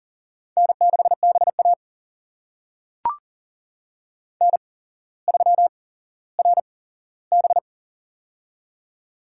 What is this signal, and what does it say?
Morse code, heard as a keyed tone, about 40 wpm, spaced as for 17 wpm, 700 Hz. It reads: N6BA N3RB
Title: USA Call Signs - 4 Characters - 8x Character Spacing - 40wpm